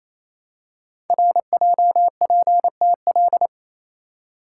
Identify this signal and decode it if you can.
Morse code, heard as a keyed tone, about 27 wpm, 700 Hz. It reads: RJPTL
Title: Code Groups - 5 Letters - 28wpm